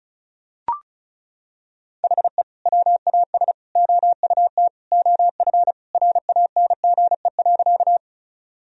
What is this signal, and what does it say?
Morse code, heard as a keyed tone, about 36 wpm, 700 Hz. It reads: HE WAS OUT OF RANGE.
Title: Sentences from Top 1000 Words - Review Part 4 - 35wpm